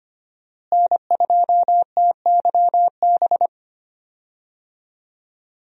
Morse code, heard as a keyed tone, about 25 wpm, 700 Hz. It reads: N2TYB
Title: USA Call Signs - 5 Characters - Repeated 3x - 25wpm